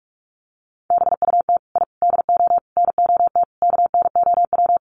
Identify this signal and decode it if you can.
Morse code, heard as a keyed tone, about 44 wpm, 700 Hz. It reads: BUT I DO NOT KNOW
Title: Sets of 5 Words 45wpm